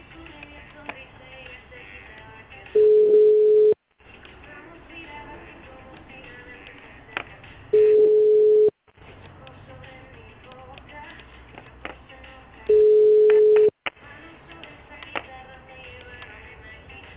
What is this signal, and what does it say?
Ringback tone (Europe, 425 Hz).